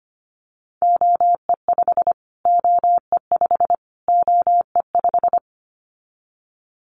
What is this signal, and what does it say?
Morse code, heard as a keyed tone, about 25 wpm, 700 Hz. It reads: OE5 OE5 OE5
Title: Callsign Prefixes for Top 25 DXCC - Repeated 3x - 25wpm